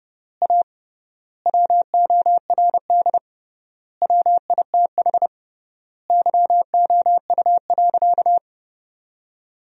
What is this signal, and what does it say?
Morse code, heard as a keyed tone, about 30 wpm, 700 Hz. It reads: A WORD WITH YOU.
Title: 3x Word-Spacing - Sentences from Top 300 Words - 30wpm